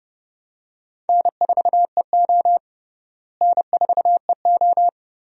Morse code, heard as a keyed tone, about 30 wpm, 700 Hz. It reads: N4EO N4EO
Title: USA Call Signs - 4 Characters - Repeated 2x - 30wpm